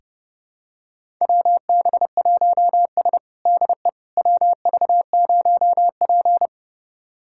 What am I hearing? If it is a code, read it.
Morse code, heard as a keyed tone, about 30 wpm, 700 Hz. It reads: WB1S DE WV0P
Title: USA Callsign DE Callsign - 4 Characters - 30wpm